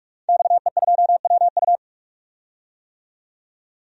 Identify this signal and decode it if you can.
Morse code, heard as a keyed tone, about 44 wpm, 700 Hz. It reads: XE2WU
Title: DXCC-100 Call Signs - 5 Characters - 45wpm